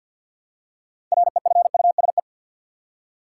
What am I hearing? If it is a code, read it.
Morse code, heard as a keyed tone, about 52 wpm, 700 Hz. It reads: REFUSE
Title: Six Letter Words - 50wpm